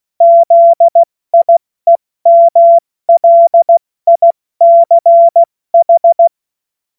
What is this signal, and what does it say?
Morse code, heard as a keyed tone, about 16 wpm, spaced as for 14 wpm, 680 Hz. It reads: ZIEMLICH
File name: Morsekurs / Witze / Witz 05 Dezember 2023 70 F80